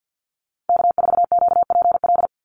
Morse code, heard as a keyed tone, about 50 wpm, 700 Hz. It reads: K4QPL
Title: Call Signs - 50wpm